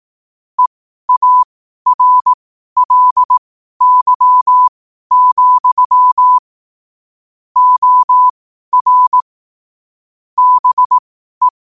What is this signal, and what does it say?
Morse code, heard as a keyed tone, about 18 wpm, spaced as for 12 wpm, 1000 Hz. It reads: EARLY, OR BE
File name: qotdfiles / 12wpm